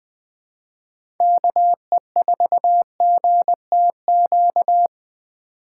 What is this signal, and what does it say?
Morse code, heard as a keyed tone, about 21 wpm, 700 Hz. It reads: KE4GTQ
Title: 2023 POTA Activator Call Signs - 20wpm